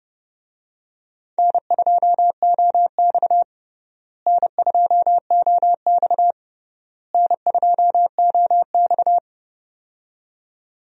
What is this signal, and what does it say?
Morse code, heard as a keyed tone, about 32 wpm, 700 Hz. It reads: N2OX N2OX N2OX
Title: USA Call Signs - 4 Characters - Repeated 3x - 30wpm